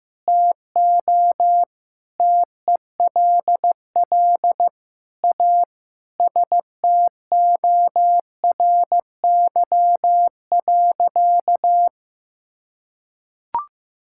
Morse code, heard as a keyed tone, about 15 wpm, 700 Hz. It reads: TO TELL A STORY.
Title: Sentences from Top 1000 Words - Review Part 6 - 15wpm